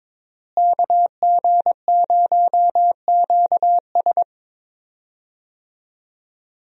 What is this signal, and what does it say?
Morse code, heard as a keyed tone, about 21 wpm, 700 Hz. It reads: KG0QS
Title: USA Call Signs - 5 Characters - Repeated 2x - 22wpm